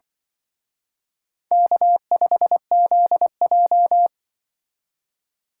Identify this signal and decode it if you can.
Morse code, heard as a keyed tone, about 24 wpm, 700 Hz. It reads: K5ZJ
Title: Speed Racing - USA 1x2 Call Signs 20wpm